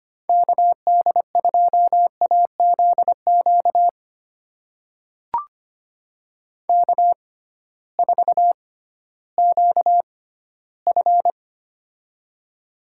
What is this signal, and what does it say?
Morse code, heard as a keyed tone, about 25 wpm, spaced as for 15 wpm, 700 Hz. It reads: KD2AZQ K4QF
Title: USA Call Signs - 6x Character Spacing - 25wpm